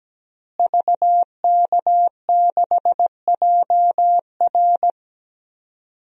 Morse code, heard as a keyed tone, about 17 wpm, 700 Hz. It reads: VK6JR
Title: DXCC-100 Call Signs - 5 Characters - Repeated 2x - 17wpm